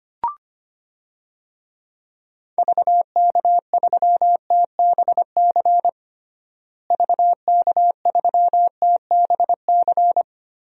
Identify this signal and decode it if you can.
Morse code, heard as a keyed tone, about 25 wpm, 700 Hz. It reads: VK3TBC VK3TBC
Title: DXCC-100 Call Signs - 6 Characters - Repeated 3x - 25wpm